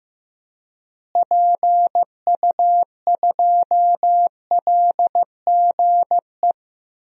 Morse code, heard as a keyed tone, about 15 wpm, 700 Hz. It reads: PU2LGE
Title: DXCC-100 Call Signs - 6 Characters - Repeated 2x - 15wpm